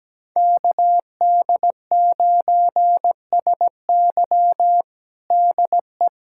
Morse code, heard as a keyed tone, about 17 wpm, 700 Hz. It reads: KD9SY DE
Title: USA Callsign DE Callsign 17wpm